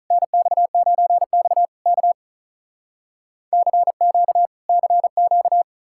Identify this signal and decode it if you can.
Morse code, heard as a keyed tone, about 48 wpm, 700 Hz. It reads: NX9X K CQ CQ